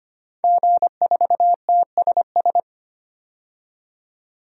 Morse code, heard as a keyed tone, about 25 wpm, 700 Hz. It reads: G4TSS
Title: DXCC-100 Call Signs - 5 Characters - 25wpm